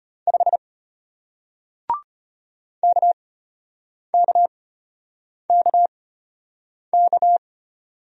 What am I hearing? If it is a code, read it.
Morse code, heard as a keyed tone, about 32 wpm, spaced as for 8 wpm, 700 Hz. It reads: 5 KKKK